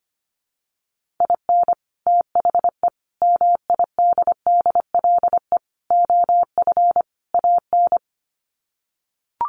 Morse code, heard as a keyed tone, about 25 wpm, 700 Hz. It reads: IN THE MIDDLE OF AN
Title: Sets of 5 Words Encore2 25wpm